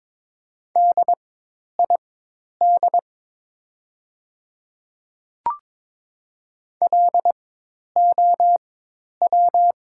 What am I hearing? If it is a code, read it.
Morse code, heard as a keyed tone, about 23 wpm, spaced as for 10 wpm, 700 Hz. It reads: DID LOW
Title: 4x Character Spacing - Top 200 Words - 22wpm